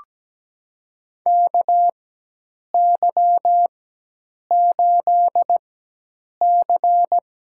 Morse code, heard as a keyed tone, about 17 wpm, spaced as for 8 wpm, 700 Hz. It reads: KY8C
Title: USA Call Signs - 4 Characters - 4x Character Spacing - 17wpm